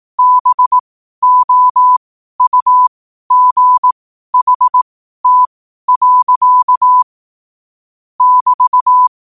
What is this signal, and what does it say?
Morse code, heard as a keyed tone, about 18 wpm, spaced as for 12 wpm, 1000 Hz. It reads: BOUGHT. <BT>